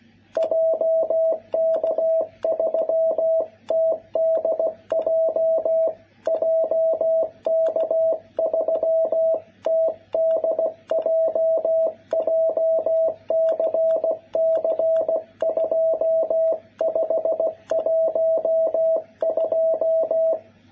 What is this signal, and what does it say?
Morse code, heard as a keyed tone, about 16 wpm, 660 Hz. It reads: JK3TDJ JK3TDJ JCC2512
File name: jh8cbh / 2021cw